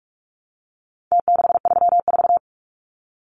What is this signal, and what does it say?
Morse code, heard as a keyed tone, about 44 wpm, 700 Hz. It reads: T634
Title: Numbers - Contest Style - 4-Digit 45wpm